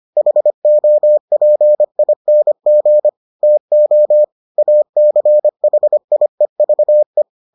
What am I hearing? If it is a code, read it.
Morse code, heard as a keyed tone, about 25 wpm, 600 Hz. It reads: HOPING TO ACHIEVE